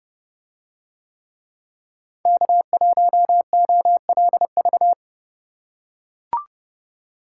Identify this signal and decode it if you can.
Morse code, heard as a keyed tone, about 30 wpm, 700 Hz. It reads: K1OLV